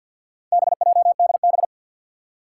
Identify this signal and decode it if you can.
Morse code, heard as a keyed tone, about 46 wpm, 700 Hz. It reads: BJDB
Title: Four Challenging Letter-Number Combinations 50wpm